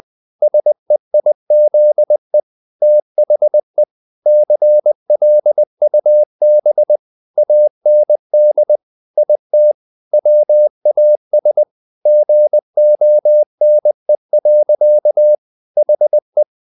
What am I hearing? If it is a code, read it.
Morse code, heard as a keyed tone, about 20 wpm, 600 Hz. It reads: SEIZE THE CLUB AND IT WAS GONE. HE
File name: the old man and the sea part 2 20wpm